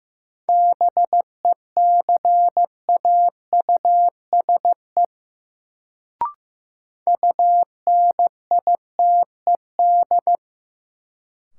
Morse code, heard as a keyed tone, about 15 wpm, 700 Hz. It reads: BECAUSE UNITED
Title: Sentences from Top 600 Words - New Word Drill - 15wpm